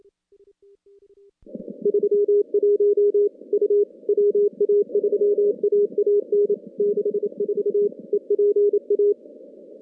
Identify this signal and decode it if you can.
Morse code, heard as a keyed tone, about 28 wpm, 410 Hz. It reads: 31 U WA3AAN 64 EPA
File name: contests / CW-SS-k5tr- -2100